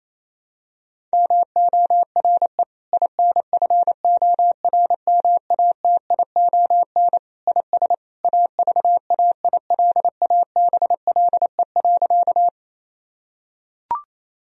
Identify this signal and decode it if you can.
Morse code, heard as a keyed tone, about 28 wpm, 700 Hz. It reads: MORE INFORMATION IS AVAILABLE.
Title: Sentences from Top 1000 Words - Review Part 3 - 28wpm